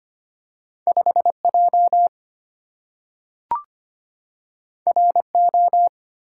Morse code, heard as a keyed tone, about 25 wpm, 700 Hz. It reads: 5J RO
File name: Two Letter-Number Combinations - 25wpm